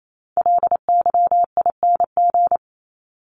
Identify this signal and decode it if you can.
Morse code, heard as a keyed tone, about 28 wpm, 700 Hz. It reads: LYING
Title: ING Suffix - Top 100 Words with Base Words 28wpm